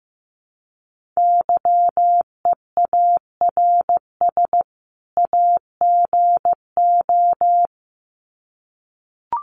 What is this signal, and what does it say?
Morse code, heard as a keyed tone, about 15 wpm, 700 Hz. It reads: YEARS AGO
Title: Sets of 2 Words 15wpm